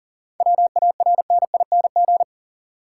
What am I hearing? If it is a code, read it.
Morse code, heard as a keyed tone, about 40 wpm, 700 Hz. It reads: WARNING
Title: Seven Letter Words - 40wpm